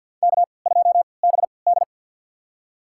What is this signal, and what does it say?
Morse code, heard as a keyed tone, about 51 wpm, spaced as for 28 wpm, 700 Hz. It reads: K2BD
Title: USA Call Signs - 4 Characters - 3x Character Spacing - 50wpm